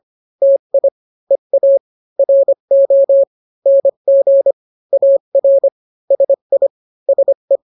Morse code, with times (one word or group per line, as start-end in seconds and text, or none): 0.4-0.9 TI
1.3-1.8 EA
2.2-3.2 RO
3.7-4.5 NG
4.9-5.7 AR
6.1-6.7 SI
7.1-7.6 SE